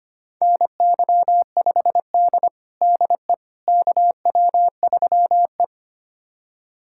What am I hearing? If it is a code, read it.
Morse code, heard as a keyed tone, about 25 wpm, 700 Hz. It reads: NY5D DE KW3E
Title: USA Callsign DE Callsign - 4 Characters - 25wpm